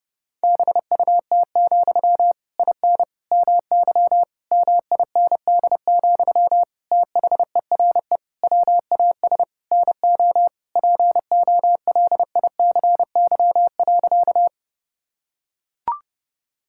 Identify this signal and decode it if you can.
Morse code, heard as a keyed tone, about 30 wpm, 700 Hz. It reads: BUT, IN MY MIND, THERE WAS NO POLICY.
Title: Sentences from Top 1000 Words - Review Part 5 - 30wpm